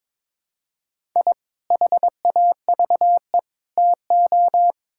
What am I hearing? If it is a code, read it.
Morse code, heard as a keyed tone, about 22 wpm, 700 Hz. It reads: I HAVE TO
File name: practice / 22wpm / Sentences with 3-Word Phrases - 22wpm